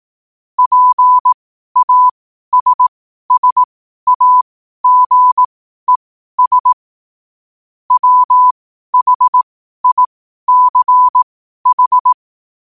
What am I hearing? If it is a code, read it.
Morse code, heard as a keyed tone, about 18 wpm, spaced as for 12 wpm, 1000 Hz. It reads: PASSAGES WHICH
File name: qotdfiles / 12wpm